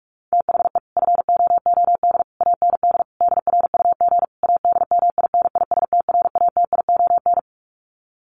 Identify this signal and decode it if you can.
Morse code, heard as a keyed tone, about 44 wpm, 700 Hz. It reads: THE FOOD AND DRUG ADMINISTRATION
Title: Sets of 5 Words 45wpm